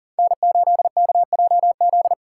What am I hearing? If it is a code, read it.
Morse code, heard as a keyed tone, about 42 wpm, 700 Hz. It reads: N8KJZ
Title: Code Groups - 5 Letter:Numbers - 40wpm